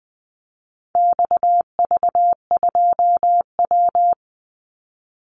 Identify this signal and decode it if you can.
Morse code, heard as a keyed tone, about 20 wpm, 700 Hz. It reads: XV2W
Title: Call Signs Repeated 3x - 20wpm